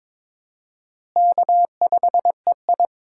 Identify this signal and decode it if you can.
Morse code, heard as a keyed tone, about 23 wpm, 700 Hz. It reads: K5EI
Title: USA Call Signs - 4 Characters - Repeated 3x - 22wpm